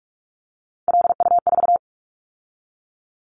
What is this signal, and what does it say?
Morse code, heard as a keyed tone, about 46 wpm, 700 Hz. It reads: LU4
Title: Callsign Prefixes for Top 50 DXCC - 45wpm